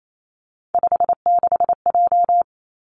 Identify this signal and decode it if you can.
Morse code, heard as a keyed tone, about 28 wpm, 700 Hz. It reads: 56J
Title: Three Challenging Letter-Number Combinations 28wpm